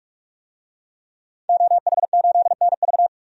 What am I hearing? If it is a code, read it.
Morse code, heard as a keyed tone, about 46 wpm, 700 Hz. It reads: OH8NV
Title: DXCC-100 Call Signs - 5 Characters - Repeated 3x - 45wpm